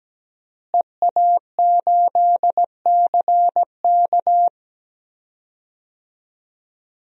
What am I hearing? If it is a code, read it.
Morse code, heard as a keyed tone, about 17 wpm, 700 Hz. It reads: EA8CK
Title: DXCC-100 Call Signs - 5 Characters - Repeated 2x - 17wpm